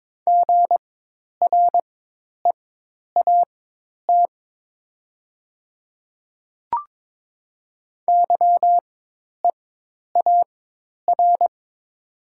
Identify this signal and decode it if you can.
Morse code, heard as a keyed tone, about 22 wpm, spaced as for 10 wpm, 700 Hz. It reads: GREAT YEAR
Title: 4x Character Spacing - Top 300 Words - 22wpm